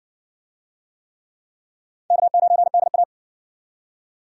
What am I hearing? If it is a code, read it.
Morse code, heard as a keyed tone, about 58 wpm, 700 Hz. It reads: K9DA